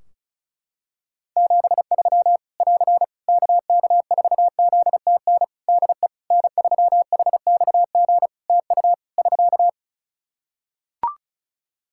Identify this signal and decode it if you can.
Morse code, heard as a keyed tone, about 34 wpm, 700 Hz. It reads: 73 <AR> KK4ZTN DE N3HXG TU <SK>
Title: Practice QSO - Exchange 4 - Ending - 35wpm